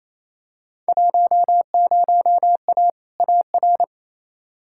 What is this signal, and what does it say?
Morse code, heard as a keyed tone, about 28 wpm, 700 Hz. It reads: 10A AR
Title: Field Day Exchanges Encore - 28wpm